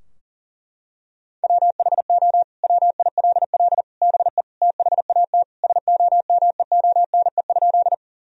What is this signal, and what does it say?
Morse code, heard as a keyed tone, about 40 wpm, 700 Hz. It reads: WHO WILL BE THAT SOMEONE?
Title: Sentences from Top 1000 Words - Review Part 4 - 40wpm